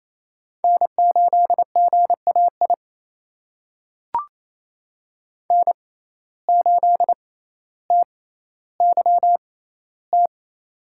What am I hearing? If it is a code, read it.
Morse code, heard as a keyed tone, about 28 wpm, spaced as for 14 wpm, 700 Hz. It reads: N8GAI N8TYT
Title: USA Call Signs - 5 Characters - 6x Character Spacing - 28wpm